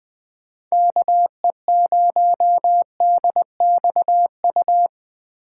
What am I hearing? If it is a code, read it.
Morse code, heard as a keyed tone, about 20 wpm, 700 Hz. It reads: KE0DXU
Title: USA Call Signs - 6 Characters - 8x Character Spacing - 20wpm